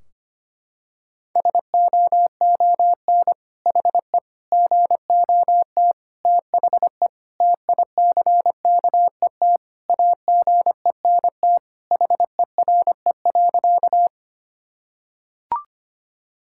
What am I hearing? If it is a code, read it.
Morse code, heard as a keyed tone, about 25 wpm, 700 Hz. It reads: SOON HE GOT THE TICKET AGENT HERE.
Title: Sentences from Top 1600 Words 25wpm